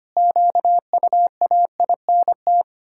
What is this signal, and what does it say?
Morse code, heard as a keyed tone, about 25 wpm, 700 Hz. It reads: QUAINT
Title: Pangram Challenge v2 - 25wpm